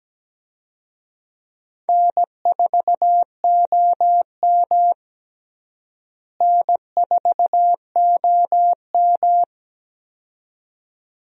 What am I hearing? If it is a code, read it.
Morse code, heard as a keyed tone, about 17 wpm, 700 Hz. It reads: N4OM N4OM
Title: USA Call Signs - 4 Characters - Repeated 2x - 17wpm